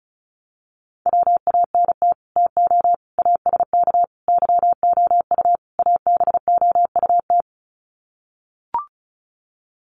Morse code, heard as a keyed tone, about 34 wpm, 700 Hz. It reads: WANT TO ASK YOU ABOUT
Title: Sets of 5 Words Encore2 35wpm